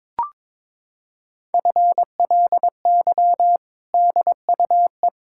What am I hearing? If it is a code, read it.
Morse code, heard as a keyed tone, about 22 wpm, 700 Hz. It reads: FLY DUE
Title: Sentences from Top 1000 Words - Review Part 6 - 22wpm